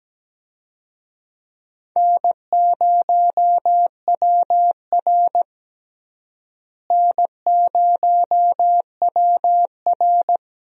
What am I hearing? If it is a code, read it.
Morse code, heard as a keyed tone, about 17 wpm, 700 Hz. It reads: N0WR N0WR
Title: USA Call Signs - 4 Characters - Repeated 2x - 17wpm